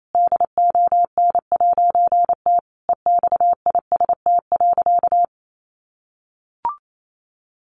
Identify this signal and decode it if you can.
Morse code, heard as a keyed tone, about 28 wpm, 700 Hz. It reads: DON'T EXIST.
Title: A Sound Of Thunder 28wpm